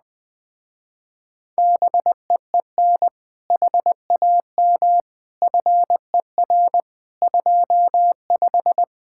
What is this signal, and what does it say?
Morse code, heard as a keyed tone, about 20 wpm, 700 Hz. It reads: BEEN HAM FER 25
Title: Speed-Racing Common QSO Phrases Encore - 17wpm